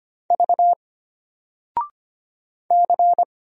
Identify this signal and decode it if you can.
Morse code, heard as a keyed tone, about 25 wpm, 700 Hz. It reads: V C
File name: Single-Letter-Number-25wpm